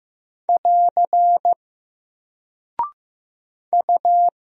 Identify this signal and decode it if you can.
Morse code, heard as a keyed tone, about 15 wpm, 700 Hz. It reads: <AR> U
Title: Single Character (Letter:Number:Punctuation:Prosign) 15wpm